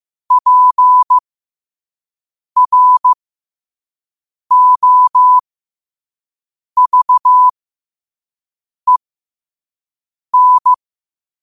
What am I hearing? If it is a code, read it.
Morse code, heard as a keyed tone, about 15 wpm, spaced as for 5 wpm, 1000 Hz. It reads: PROVEN